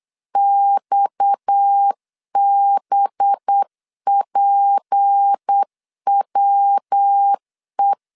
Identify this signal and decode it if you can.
Morse code, heard as a keyed tone, about 8 wpm, 800 Hz. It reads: XBPWE
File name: LEZIONI / 6 PKX0 / 6 PKX 8wpm 3041